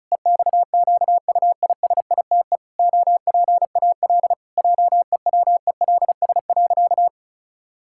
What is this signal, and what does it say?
Morse code, heard as a keyed tone, about 36 wpm, 700 Hz. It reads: EXQUISITE OPAL JEWELS.
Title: Pangram Challenge v2 - 35wpm